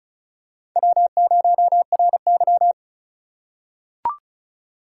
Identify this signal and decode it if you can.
Morse code, heard as a keyed tone, about 36 wpm, 700 Hz. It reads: W0RY
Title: USA Call Signs - 3x Character Spacing - 35wpm